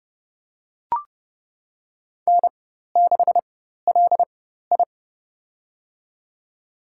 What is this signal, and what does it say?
Morse code, heard as a keyed tone, about 30 wpm, spaced as for 14 wpm, 700 Hz. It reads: N6LI
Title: USA Call Signs - 4 Characters - 4x Character Spacing - 30wpm